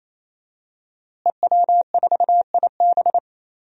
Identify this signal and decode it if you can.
Morse code, heard as a keyed tone, about 28 wpm, 700 Hz. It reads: EW4IB
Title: DXCC-100 Call Signs - 5 Characters - Repeated 3x - 28wpm